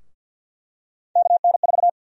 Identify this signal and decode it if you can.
Morse code, heard as a keyed tone, about 52 wpm, 700 Hz. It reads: KN4
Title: USA Call Sign Prefixes - Repeated 2x - 50wpm